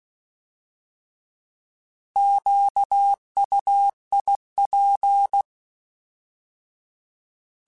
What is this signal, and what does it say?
Morse code, heard as a keyed tone, about 16 wpm, 800 Hz. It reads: QUIP